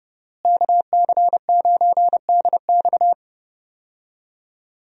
Morse code, heard as a keyed tone, about 30 wpm, 700 Hz. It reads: KC9DX
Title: USA Call Signs - 5 Characters - Repeated 2x - 30wpm